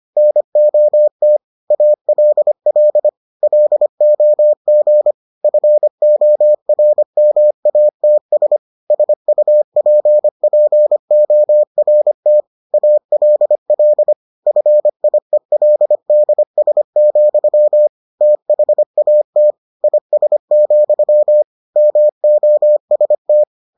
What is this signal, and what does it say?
Morse code, heard as a keyed tone, about 25 wpm, 600 Hz. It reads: NOT ALL LOG FORMATS SUPPORT ALL FIELDS, THAT IS, MOST